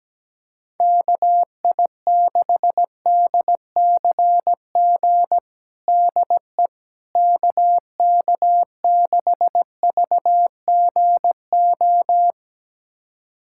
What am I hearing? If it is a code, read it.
Morse code, heard as a keyed tone, about 17 wpm, 700 Hz. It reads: KI6DCG DE KK6VGO
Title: USA Callsign DE Callsign - 6 Characters - 17wpm